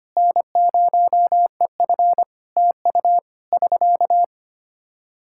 Morse code, heard as a keyed tone, about 25 wpm, 700 Hz. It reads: N0EF TU <SK>
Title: Practice QSO - Exchange 4 - Ending - 25wpm